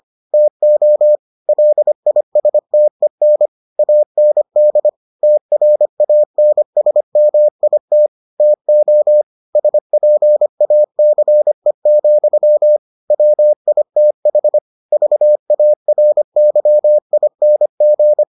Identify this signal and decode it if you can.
Morse code, heard as a keyed tone, about 25 wpm, 600 Hz. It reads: TO LISTEN AND TRANSMIT TO SPACE, WITH VARYING